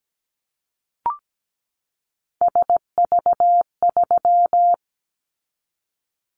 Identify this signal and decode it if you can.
Morse code, heard as a keyed tone, about 17 wpm, 700 Hz. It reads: SV3
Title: Callsign Prefixes for Top 50 DXCC - 17wpm